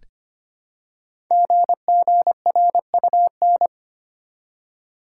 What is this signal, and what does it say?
Morse code, heard as a keyed tone, about 25 wpm, 700 Hz. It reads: GGRUN
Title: Code Groups - 5 Letters - 25wpm